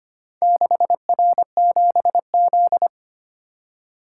Morse code, heard as a keyed tone, about 25 wpm, 700 Hz. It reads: 6R7Z
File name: practice / 25wpm / Four Letter-Number Combinations - 25wpm